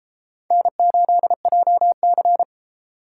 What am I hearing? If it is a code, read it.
Morse code, heard as a keyed tone, about 33 wpm, 700 Hz. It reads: N8JC